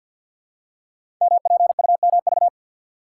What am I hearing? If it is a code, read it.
Morse code, heard as a keyed tone, about 46 wpm, 700 Hz. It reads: MPUMV
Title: Code Groups - 5 Letters - 50wpm